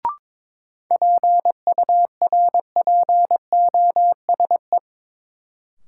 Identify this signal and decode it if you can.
Morse code, heard as a keyed tone, about 22 wpm, 700 Hz. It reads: PURPOSE
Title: Sentences from Top 800 Words - New Word Drill - 22wpm